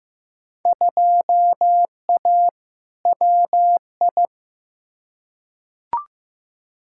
Morse code, heard as a keyed tone, about 15 wpm, 700 Hz. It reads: 2A WI
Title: Field Day Exchanges Encore - 15wpm